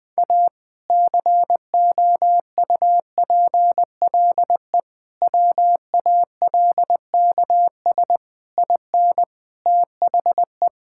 Morse code, heard as a keyed tone, about 20 wpm, 700 Hz. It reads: A COUPLE WALKS IN THE
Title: Sentences with 2-Word Phrases - 20wpm